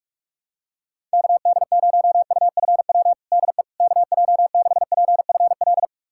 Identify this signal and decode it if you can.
Morse code, heard as a keyed tone, about 46 wpm, 700 Hz. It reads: KD0UFW DE KJ6PFL
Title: USA Callsign DE Callsign - 6 Characters - 45wpm